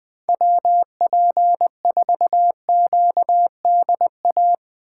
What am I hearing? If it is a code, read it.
Morse code, heard as a keyed tone, about 21 wpm, 700 Hz. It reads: WP4QDA
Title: USA Call Signs - 6 Characters - 2x Character Spacing - 20wpm